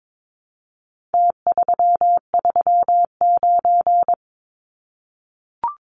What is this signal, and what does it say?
Morse code, heard as a keyed tone, about 22 wpm, 700 Hz. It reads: T339
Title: Numbers - Contest Style - 4-Digit 22wpm